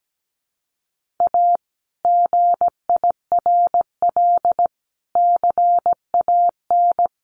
Morse code, heard as a keyed tone, about 17 wpm, 700 Hz. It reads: A GIRL CAN
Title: Sentences from Top 1400 Words 17wpm